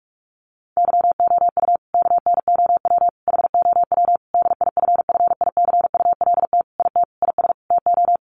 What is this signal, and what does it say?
Morse code, heard as a keyed tone, about 44 wpm, 700 Hz. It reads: YOU KNOW HOW DIFFICULT IT IS TO
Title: Sentences from Top 1100 Words 45wpm